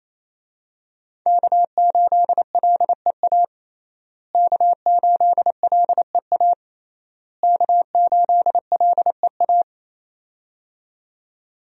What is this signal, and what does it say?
Morse code, heard as a keyed tone, about 28 wpm, 700 Hz. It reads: K8LEA K8LEA K8LEA
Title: USA Call Signs - 5 Characters - Repeated 3x - 28wpm